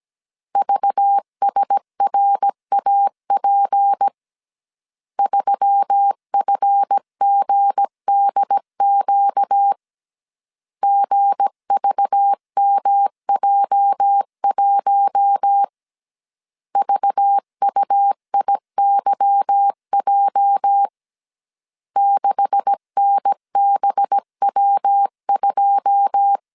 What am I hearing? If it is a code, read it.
Morse code, heard as a keyed tone, about 17 wpm, 800 Hz. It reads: VSRAP 3FGDQ GVMJ1 VUIYJ 6NBW2
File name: CW / DET / 1 9 1500C S C 16wpm 2909